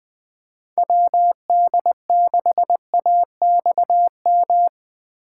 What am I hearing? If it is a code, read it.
Morse code, heard as a keyed tone, about 20 wpm, 700 Hz. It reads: WD6AXM
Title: Speed Racing - USA 2x3 Call Signs 17wpm